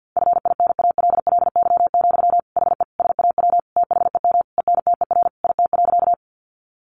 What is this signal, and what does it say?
Morse code, heard as a keyed tone, about 50 wpm, 700 Hz. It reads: FINALLY, HE SAW THEM ENTER IT.
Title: Sentences from Top 1300 Words 50wpm